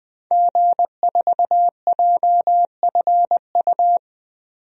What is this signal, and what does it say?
Morse code, heard as a keyed tone, about 20 wpm, 700 Hz. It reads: G4JFU
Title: DXCC-100 Call Signs - 5 Characters - 20wpm